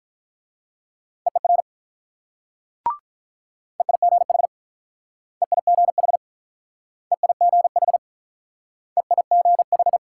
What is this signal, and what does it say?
Morse code, heard as a keyed tone, about 48 wpm, 700 Hz. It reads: EER EIGH EIGH EIGH EIGH